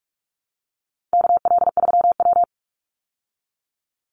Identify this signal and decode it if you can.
Morse code, heard as a keyed tone, about 44 wpm, 700 Hz. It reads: KL3W
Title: Call Signs Repeated 3x - 45wpm